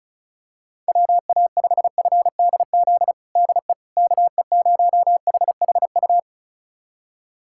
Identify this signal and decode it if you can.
Morse code, heard as a keyed tone, about 36 wpm, 700 Hz. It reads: WA5FDZ DE KE0HHU
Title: USA Callsign DE Callsign 35wpm